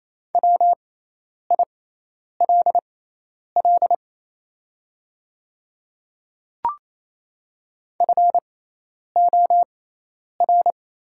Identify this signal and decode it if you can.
Morse code, heard as a keyed tone, about 28 wpm, spaced as for 10 wpm, 690 Hz. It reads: WILL FOR
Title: 6x Character Spacing - Top 100 Words - 28wpm